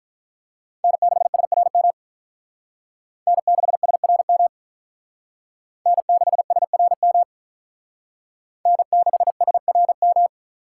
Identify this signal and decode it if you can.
Morse code, heard as a keyed tone, about 44 wpm, 700 Hz. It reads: N6SRM N6SRM N6SRM N6SRM